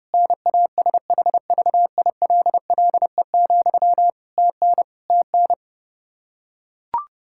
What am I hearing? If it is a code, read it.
Morse code, heard as a keyed tone, about 30 wpm, 700 Hz. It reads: NASHVILLE, TN TN
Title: Practice QSO - Exchange 1 - Beginning - 30wpm